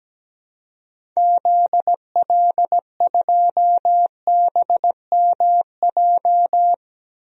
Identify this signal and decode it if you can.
Morse code, heard as a keyed tone, about 17 wpm, 700 Hz. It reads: ZL2BMJ
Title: DXCC-100 Call Signs - 6 Characters - Repeated 3x - 17wpm